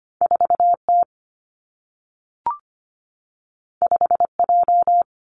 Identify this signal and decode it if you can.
Morse code, heard as a keyed tone, about 25 wpm, 700 Hz. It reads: 4 T 5 J
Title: Two Challenging Letter-Number Combinations 25wpm